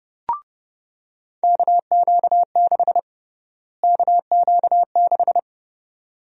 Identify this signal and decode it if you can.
Morse code, heard as a keyed tone, about 30 wpm, 700 Hz. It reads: KQ6 KQ6
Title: USA Call Sign Prefixes - Repeated 2x - 30wpm